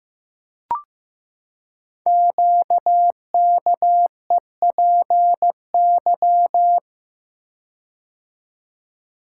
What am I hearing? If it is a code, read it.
Morse code, heard as a keyed tone, about 15 wpm, 700 Hz. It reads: QKEPY
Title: Code Groups - 5 Letters - 15wpm